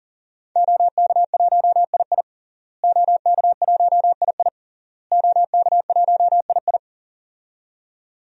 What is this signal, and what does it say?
Morse code, heard as a keyed tone, about 40 wpm, 700 Hz. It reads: OK1II OK1II OK1II
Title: DXCC-100 Call Signs - 5 Characters - Repeated 3x - 40wpm